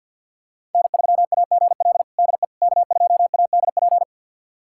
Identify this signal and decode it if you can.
Morse code, heard as a keyed tone, about 52 wpm, 700 Hz. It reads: N3AGL DE K2ADP
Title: USA Callsign DE Callsign - 5 Characters - 50wpm